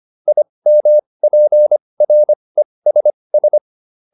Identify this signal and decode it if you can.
Morse code, heard as a keyed tone, about 25 wpm, spaced as for 20 wpm, 600 Hz. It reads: IMPRESS